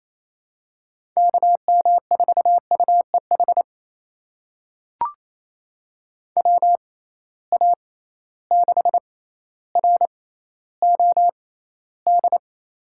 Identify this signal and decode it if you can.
Morse code, heard as a keyed tone, about 28 wpm, spaced as for 14 wpm, 700 Hz. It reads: KM4UEH WA6ROD
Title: USA Call Signs - 6 Characters - 6x Character Spacing - 28wpm